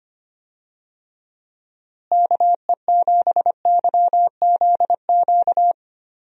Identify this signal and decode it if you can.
Morse code, heard as a keyed tone, about 25 wpm, 700 Hz. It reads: KE7YZQ
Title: USA Call Signs - 6 Characters - Repeated 2x - 25wpm